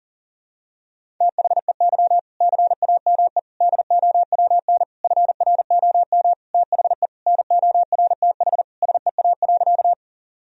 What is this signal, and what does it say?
Morse code, heard as a keyed tone, about 40 wpm, 700 Hz. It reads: THEY CAME DOWN FROM THE NORTH SEA.
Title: Sentences from Top 1000 Words - Review Part 1 - 40wpm